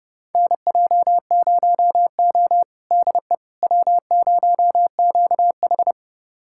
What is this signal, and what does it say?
Morse code, heard as a keyed tone, about 30 wpm, 700 Hz. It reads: NJ0O DE W0QH
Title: USA Callsign DE Callsign - 4 Characters - 30wpm